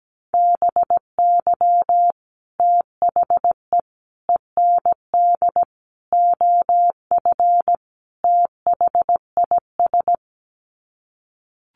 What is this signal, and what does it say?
Morse code, heard as a keyed tone, about 17 wpm, 700 Hz. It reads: BY THE END OF THIS
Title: Sets of 5 Words Encore 17wpm